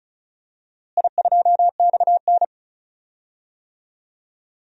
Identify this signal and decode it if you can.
Morse code, heard as a keyed tone, about 36 wpm, 700 Hz. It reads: I2XN
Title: DXCC-100 Call Signs - 4 Characters - Repeated 2x - 35wpm